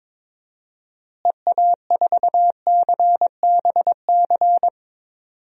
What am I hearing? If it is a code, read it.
Morse code, heard as a keyed tone, about 22 wpm, 700 Hz. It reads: EA4CBC
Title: DXCC-100 Call Signs - 6 Characters - Repeated 2x - 22wpm